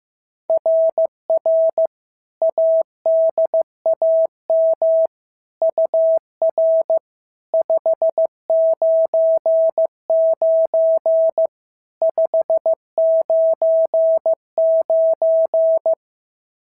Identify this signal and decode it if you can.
Morse code, heard as a keyed tone, about 14 wpm, 650 Hz. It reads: RR ADAM UR 599 599
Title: SOTA QSO Practice 15wpm